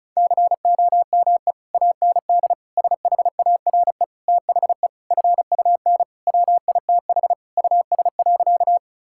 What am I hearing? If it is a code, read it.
Morse code, heard as a keyed tone, about 36 wpm, 700 Hz. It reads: COME AND SHARE THE FUN WITH US.
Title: Sentences from Top 1000 Words - Review Part 6 - 35wpm